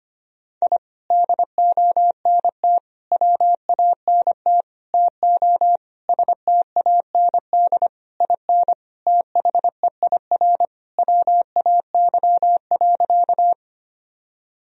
Morse code, heard as a keyed tone, about 25 wpm, 700 Hz. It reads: I DONT WANT TO STAND IN THEIR WAY.
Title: Sentences from Top 1000 Words - Review Part 1 - 25wpm